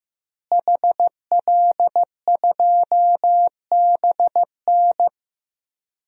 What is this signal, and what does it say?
Morse code, heard as a keyed tone, about 15 wpm, 700 Hz. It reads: HL2BN
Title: DXCC-100 Call Signs - 5 Characters - Repeated 2x - 15wpm